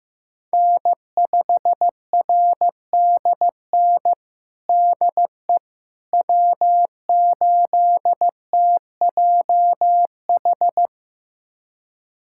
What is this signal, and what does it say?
Morse code, heard as a keyed tone, about 15 wpm, 700 Hz. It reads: N5RDN DE W8TJH
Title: USA Callsign DE Callsign - 5 Characters - 15wpm